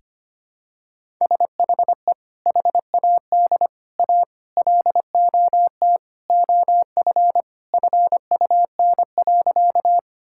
Morse code, heard as a keyed tone, about 25 wpm, 700 Hz. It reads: SHE HAD A LOT OF FUN.
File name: practice / 25wpm / Sentences with 4-Word Phrases - 25wpm